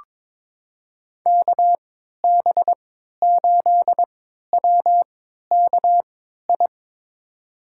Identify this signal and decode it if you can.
Morse code, heard as a keyed tone, about 22 wpm, spaced as for 13 wpm, 700 Hz. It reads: KB8WKI
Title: USA Call Signs - 6 Characters - 3x Character Spacing - 22wpm